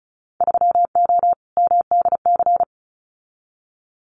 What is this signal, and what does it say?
Morse code, heard as a keyed tone, about 34 wpm, 700 Hz. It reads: 3O MDC
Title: Winter Field Day Exchanges Encore 35wpm